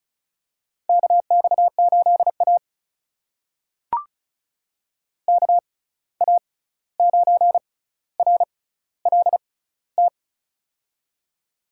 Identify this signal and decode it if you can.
Morse code, heard as a keyed tone, about 36 wpm, spaced as for 16 wpm, 700 Hz. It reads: KX8A KA9RLT
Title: USA Call Signs - 6x Character Spacing - 35wpm